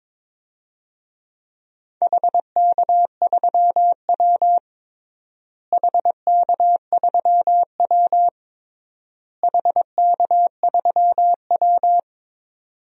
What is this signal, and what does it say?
Morse code, heard as a keyed tone, about 22 wpm, 700 Hz. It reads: HK3W HK3W HK3W
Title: DXCC-100 Call Signs - 4 Characters - Repeated 3x - 22wpm